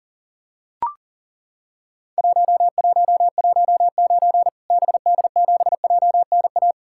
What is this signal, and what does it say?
Morse code, heard as a keyed tone, about 40 wpm, 700 Hz. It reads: 1119 BD7JNA
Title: Worked All Europe DX Contest - QTC Exchange - 40wpm